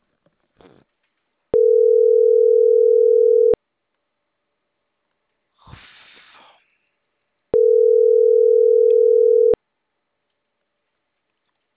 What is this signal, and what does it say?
Ringback tone (North America), 440 + 480 Hz.